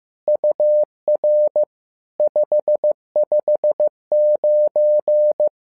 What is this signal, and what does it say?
Morse code, heard as a keyed tone, about 15 wpm, 600 Hz. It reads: UR 559
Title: SOTA QSO Practice 15wpm